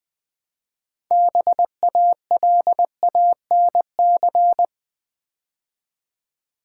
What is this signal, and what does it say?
Morse code, heard as a keyed tone, about 20 wpm, 700 Hz. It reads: BALANC
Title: Build A Word - 3 Letters At A Time - 8-Letter Words - 20wpm